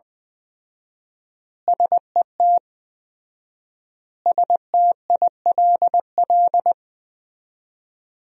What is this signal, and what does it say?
Morse code, heard as a keyed tone, about 20 wpm, 700 Hz. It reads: SET STILL
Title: 4x Word-Spacing - Sentences from Top 300 Words - 20wpm